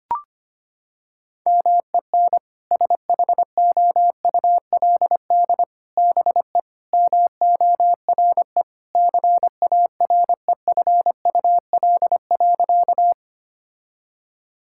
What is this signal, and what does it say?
Morse code, heard as a keyed tone, about 25 wpm, 700 Hz. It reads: MEN SHOULD BE MORE CAREFUL.
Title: Sentences from Top 1600 Words 25wpm